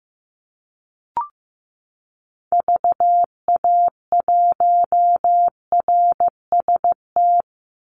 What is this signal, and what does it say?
Morse code, heard as a keyed tone, about 15 wpm, 700 Hz. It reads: VA1RST
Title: Call Signs Repeated 2x - 15wpm